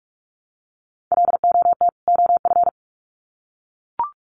Morse code, heard as a keyed tone, about 44 wpm, 700 Hz. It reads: LOT OF
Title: Sets of 2 Words 45wpm